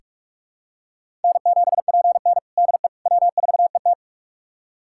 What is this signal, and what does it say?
Morse code, heard as a keyed tone, about 46 wpm, 700 Hz. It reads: N7PN DE W4ET
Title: USA Callsign DE Callsign - 4 Characters - 45wpm